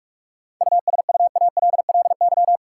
Morse code, heard as a keyed tone, about 47 wpm, 700 Hz. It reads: USUALLY
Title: Seven Letter Words - 45wpm